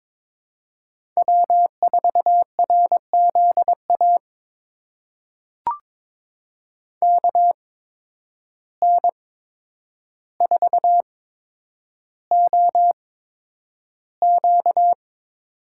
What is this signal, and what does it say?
Morse code, heard as a keyed tone, about 22 wpm, spaced as for 9 wpm, 700 Hz. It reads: W4RZA KN4OQ
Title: USA Call Signs - 5 Characters - 8x Character Spacing - 22wpm